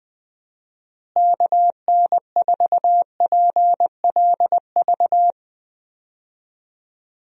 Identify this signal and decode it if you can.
Morse code, heard as a keyed tone, about 21 wpm, 700 Hz. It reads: KN4PLV
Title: USA Call Signs - 6 Characters - Repeated 3x - 20wpm